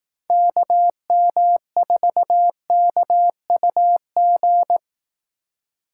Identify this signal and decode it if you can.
Morse code, heard as a keyed tone, about 19 wpm, 700 Hz. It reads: KM4KUG